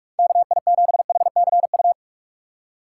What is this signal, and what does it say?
Morse code, heard as a keyed tone, about 44 wpm, 700 Hz. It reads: KI7HCU
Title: USA Call Signs - 6 Characters - 6x Character Spacing - 45wpm